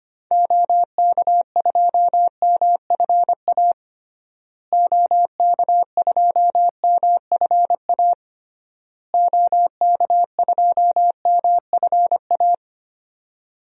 Morse code, heard as a keyed tone, about 25 wpm, 700 Hz. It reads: OK2MFA OK2MFA OK2MFA
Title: DXCC-100 Call Signs - 6 Characters - Repeated 3x - 25wpm